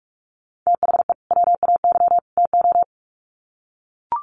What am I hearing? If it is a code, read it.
Morse code, heard as a keyed tone, about 44 wpm, 700 Hz. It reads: THE WAY TO
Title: Sets of 3 Words 45wpm